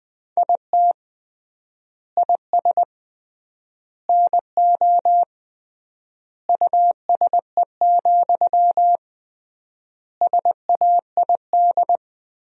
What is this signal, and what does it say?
Morse code, heard as a keyed tone, about 20 wpm, 700 Hz. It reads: IT IS NO USE, SAID
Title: 3x Word-Spacing - Sentences from Top 500 Words - 20wpm